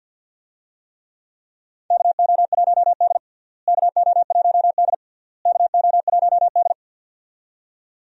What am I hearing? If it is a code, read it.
Morse code, heard as a keyed tone, about 52 wpm, 700 Hz. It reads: KO1D KO1D KO1D
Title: USA Call Signs - 4 Characters - Repeated 3x - 50wpm